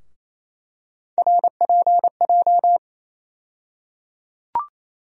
Morse code, heard as a keyed tone, about 28 wpm, 700 Hz. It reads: RPJ
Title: USA Call Sign Postfixes - 3 Letters 28wpm